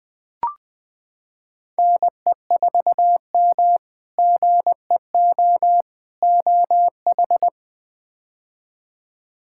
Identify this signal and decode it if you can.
Morse code, heard as a keyed tone, about 21 wpm, 700 Hz. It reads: NE4M GEO OH
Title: K1USN SST QSO Practice - 20wpm